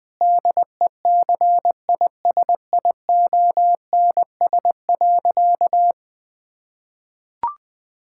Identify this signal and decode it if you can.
Morse code, heard as a keyed tone, about 21 wpm, 700 Hz. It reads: DECISIONS.
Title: Difficult Sentences with Hints - 20wpm